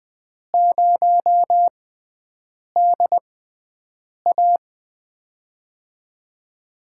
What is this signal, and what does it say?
Morse code, heard as a keyed tone, about 20 wpm, spaced as for 7 wpm, 700 Hz. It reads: 0DA
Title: USA Call Signs - 5 Characters - 6x Character Spacing - 20wpm